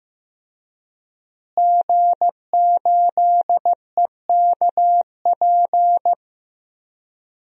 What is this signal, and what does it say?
Morse code, heard as a keyed tone, about 15 wpm, 700 Hz. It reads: G8EKP
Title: DXCC-100 Call Signs - 5 Characters - Repeated 3x - 15wpm